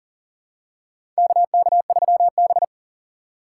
Morse code, heard as a keyed tone, about 41 wpm, 690 Hz. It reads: KK3B
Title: USA Call Signs - 4 Characters - 2x Character Spacing - 40wpm